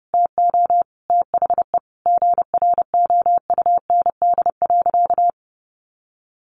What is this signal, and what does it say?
Morse code, heard as a keyed tone, about 28 wpm, 700 Hz. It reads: TO THE GROUND.
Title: Sentences from Top 500 Words 30wpm